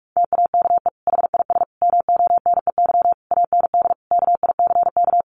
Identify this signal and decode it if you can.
Morse code, heard as a keyed tone, about 44 wpm, 700 Hz. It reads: TAKE HIS MONEY AND KICK
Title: Sentences from Top 1300 Words 45wpm